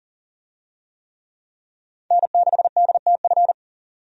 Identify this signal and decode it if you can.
Morse code, heard as a keyed tone, about 40 wpm, 700 Hz. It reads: N6DTF
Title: USA Call Signs - 5 Characters - 40wpm